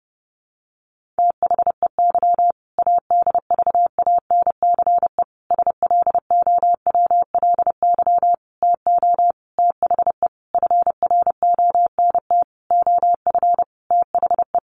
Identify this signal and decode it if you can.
Morse code, heard as a keyed tone, about 30 wpm, 700 Hz. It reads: THEY ADVANCE SLOWLY TO THE FRONT OF THE
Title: Sentences from Top 1400 Words 30wpm